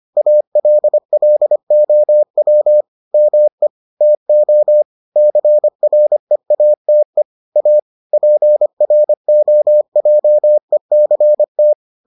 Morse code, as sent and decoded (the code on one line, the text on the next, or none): .- .-.. .-.. --- .-- / -- . / - --- / -.-. .-. .- - . / .- / .--. .-. --- .--- . -.-. -
ALLOW ME TO CREATE A PROJECT